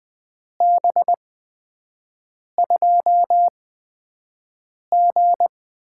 Morse code, heard as a keyed tone, about 20 wpm, spaced as for 5 wpm, 700 Hz. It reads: B2G
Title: USA Call Signs - 5 Characters - 8x Character Spacing - 20wpm